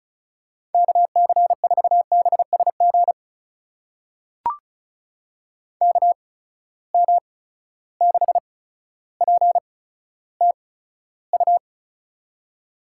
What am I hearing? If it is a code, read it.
Morse code, heard as a keyed tone, about 36 wpm, spaced as for 15 wpm, 700 Hz. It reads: KC4BSG KM6PTU
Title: USA Call Signs - 6 Characters - 8x Character Spacing - 35wpm